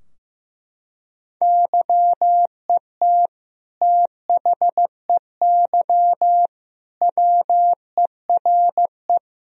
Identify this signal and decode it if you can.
Morse code, heard as a keyed tone, about 16 wpm, 700 Hz. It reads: YET THEY WERE
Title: Sentences from Top 1000 Words - Review Part 3 - 15wpm